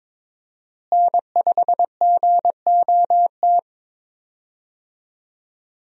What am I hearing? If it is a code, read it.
Morse code, heard as a keyed tone, about 21 wpm, 700 Hz. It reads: N5GOT
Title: USA Call Signs - 5 Characters - Repeated 2x - 22wpm